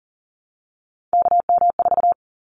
Morse code, heard as a keyed tone, about 40 wpm, 700 Hz. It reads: KM4
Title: Callsign Prefixes for Top 25 DXCC - Repeated 3x - 40wpm